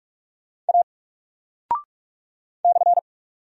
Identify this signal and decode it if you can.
Morse code, heard as a keyed tone, about 44 wpm, 700 Hz. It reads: A /
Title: Single Character (Letter:Number:Punctuation:Prosign) 45wpm